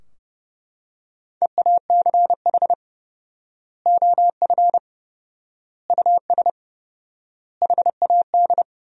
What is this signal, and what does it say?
Morse code, heard as a keyed tone, about 30 wpm, 700 Hz. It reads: EACH OF US HAD
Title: 4x Word-Spacing - Sentences from Top 500 Words - 30wpm